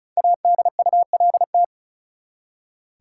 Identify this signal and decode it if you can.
Morse code, heard as a keyed tone, about 36 wpm, 700 Hz. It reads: ADULT
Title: Five Letter Words - 35wpm